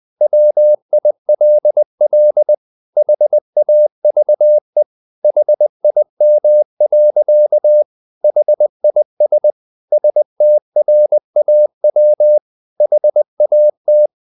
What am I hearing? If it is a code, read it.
Morse code, heard as a keyed tone, about 20 wpm, 600 Hz. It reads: WILL HAVE HIM. HIS STRAW HAT
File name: the old man and the sea part 2 20wpm